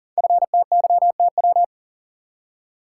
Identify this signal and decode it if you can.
Morse code, heard as a keyed tone, about 42 wpm, 690 Hz. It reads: FTYTW
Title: Code Groups - 5 Letters - 40wpm